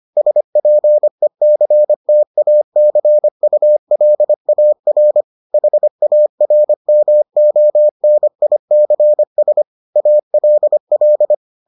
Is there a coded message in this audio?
Yes: Morse code from 0.2 to 11.3 s, heard as a keyed tone, about 25 wpm, 600 Hz. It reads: SPECTACULAR HARMONICS ALL